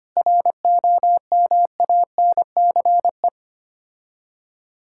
Morse code, heard as a keyed tone, about 25 wpm, 700 Hz. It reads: ROMANCE
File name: practice / 25wpm / Difficult Words with Hints - 25wpm